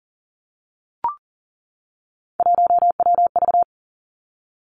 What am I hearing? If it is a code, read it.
Morse code, heard as a keyed tone, about 40 wpm, 700 Hz. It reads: 1WV
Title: Three Challenging Letter-Number Combinations 40wpm